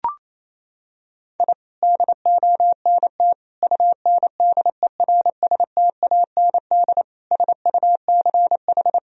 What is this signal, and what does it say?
Morse code, heard as a keyed tone, about 28 wpm, 700 Hz. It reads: I DONT UNDERSTAND SUCH
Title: Sentences from Top 1000 Words - Review Part 3 - 28wpm